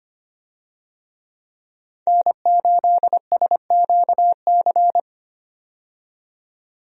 Morse code, heard as a keyed tone, about 25 wpm, 700 Hz. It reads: N8SQC
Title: USA Call Signs - 5 Characters - 25wpm